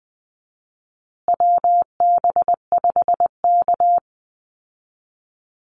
Morse code, heard as a keyed tone, about 20 wpm, 700 Hz. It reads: WB5K
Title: Call Signs Repeated 2x - 20wpm